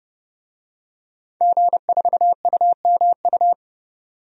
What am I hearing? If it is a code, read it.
Morse code, heard as a keyed tone, about 30 wpm, 700 Hz. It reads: G4UMU
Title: DXCC-100 Call Signs - 5 Characters - 30wpm